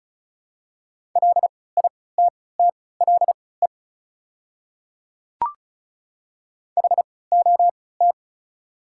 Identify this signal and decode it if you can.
Morse code, heard as a keyed tone, about 34 wpm, spaced as for 20 wpm, 690 Hz. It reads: LITTLE HOT
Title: 3x Character Spacing - Top 500 Words - 35wpm